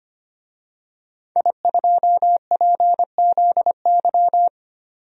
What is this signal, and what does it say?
Morse code, heard as a keyed tone, about 25 wpm, 700 Hz. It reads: I2PZY
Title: DXCC-100 Call Signs - 5 Characters - 25wpm